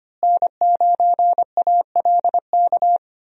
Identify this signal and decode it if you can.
Morse code, heard as a keyed tone, about 25 wpm, 700 Hz. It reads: N9ALK
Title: USA Call Signs - 5 Characters - Repeated 3x - 25wpm